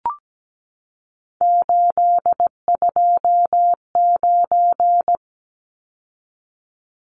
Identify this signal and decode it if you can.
Morse code, heard as a keyed tone, about 17 wpm, 700 Hz. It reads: 829
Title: Numbers - 3-Digit 17wpm